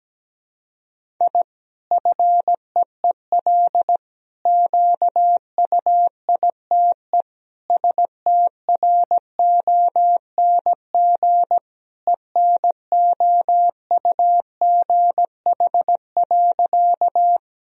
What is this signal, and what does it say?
Morse code, heard as a keyed tone, about 17 wpm, 700 Hz. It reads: I FEEL QUITE STRONG ENOUGH.
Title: Sentences from Top 1000 Words - Review Part 3 - 17wpm